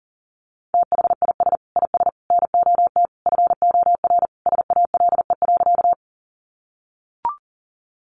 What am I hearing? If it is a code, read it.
Morse code, heard as a keyed tone, about 40 wpm, 700 Hz. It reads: THIS IS NOT FOR SALE.
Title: Sentences from Top 1400 Words 40wpm